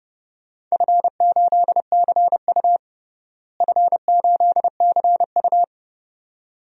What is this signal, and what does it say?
Morse code, heard as a keyed tone, about 32 wpm, 700 Hz. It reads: F8CU F8CU
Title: DXCC-100 Call Signs - 4 Characters - Repeated 2x - 30wpm